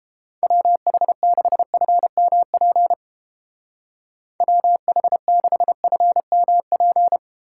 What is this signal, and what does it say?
Morse code, heard as a keyed tone, about 29 wpm, 700 Hz. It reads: WH6FMP WH6FMP